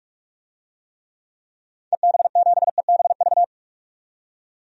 Morse code, heard as a keyed tone, about 47 wpm, 700 Hz. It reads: EB7EBV
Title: DXCC-100 Call Signs - 6 Characters - 45wpm